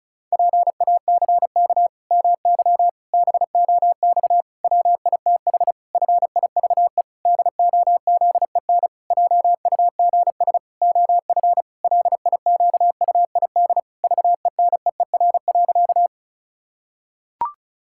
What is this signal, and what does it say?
Morse code, heard as a keyed tone, about 36 wpm, 700 Hz. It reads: PACK MY BOX WITH FIVE DOZEN JUGS OF LIQUID VENEER.
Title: Pangram Challenge v2 - 35wpm